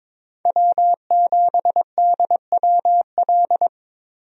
Morse code, heard as a keyed tone, about 22 wpm, 700 Hz. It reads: W7DWL
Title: USA Call Signs - 5 Characters - Repeated 2x - 22wpm